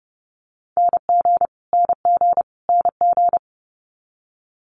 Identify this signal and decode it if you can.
Morse code, heard as a keyed tone, about 30 wpm, 700 Hz. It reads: NG NG NG
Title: Multi-letter Phonemes Repeated 3x 30wpm